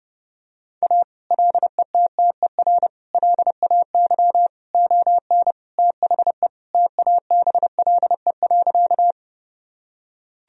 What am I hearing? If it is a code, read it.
Morse code, heard as a keyed tone, about 32 wpm, 700 Hz. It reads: A LETTER LAY ON THE TABLE.
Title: Sentences from Top 1000 Words - Review Part 2 - 30wpm